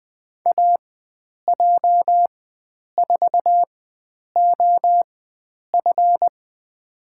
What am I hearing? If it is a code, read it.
Morse code, heard as a keyed tone, about 20 wpm, spaced as for 9 wpm, 700 Hz. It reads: AJ4OF
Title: USA Call Signs - 5 Characters - 4x Character Spacing - 20wpm